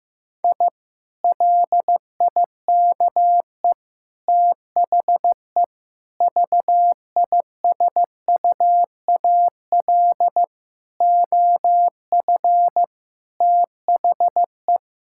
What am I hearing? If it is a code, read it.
Morse code, heard as a keyed tone, about 15 wpm, 700 Hz. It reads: I LIKE THE VISUAL OF THE
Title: Sentences from Top 1000 Words - Review Part 6 - 15wpm